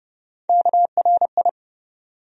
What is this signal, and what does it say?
Morse code, heard as a keyed tone, about 30 wpm, 700 Hz. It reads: KRI
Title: USA Call Sign Postfixes - 3 Letters 30wpm